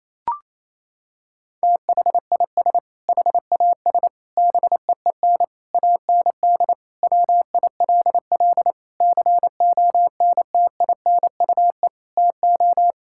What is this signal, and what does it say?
Morse code, heard as a keyed tone, about 28 wpm, 700 Hz. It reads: THIS HAS BEEN AND WILL CONTINUE TO
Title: Sentences from Top 1600 Words 28wpm